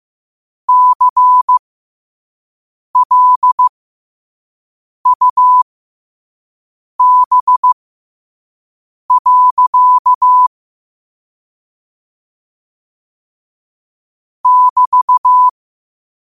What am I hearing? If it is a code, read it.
Morse code, heard as a keyed tone, about 15 wpm, spaced as for 5 wpm, 1000 Hz. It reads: CLUB. <BT>
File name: qotdfiles / 5wpm